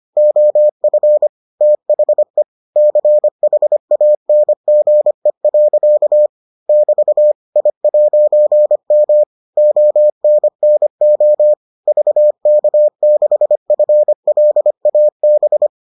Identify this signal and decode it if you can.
Morse code, heard as a keyed tone, about 25 wpm, 600 Hz. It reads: OF THE CHANGE. <BT> I'M ONNO VK6FLAB